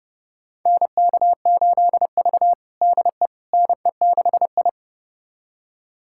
Morse code, heard as a keyed tone, about 30 wpm, 700 Hz. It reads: NK8V DE NE6I
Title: USA Callsign DE Callsign - 4 Characters - 30wpm